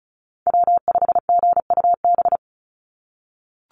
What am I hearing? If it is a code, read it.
Morse code, heard as a keyed tone, about 36 wpm, 700 Hz. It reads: W5GUB
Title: USA Call Signs 35wpm